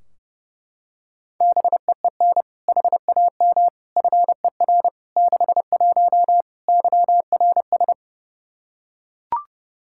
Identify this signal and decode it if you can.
Morse code, heard as a keyed tone, about 30 wpm, 700 Hz. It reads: BEEN HAM FER 61 YRS
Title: Practice QSO - Exchange 3 - More Info - 30wpm